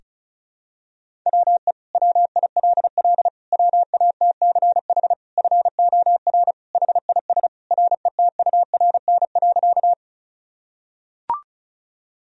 Morse code, heard as a keyed tone, about 34 wpm, 700 Hz. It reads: WE WILL WATCH FOR HIS RETURN.
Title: Sentences from Top 1000 Words - Review Part 1 - 35wpm